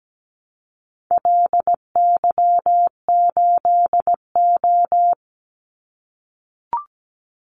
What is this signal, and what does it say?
Morse code, heard as a keyed tone, about 17 wpm, 700 Hz. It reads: LY8O
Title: Call Signs - 17wpm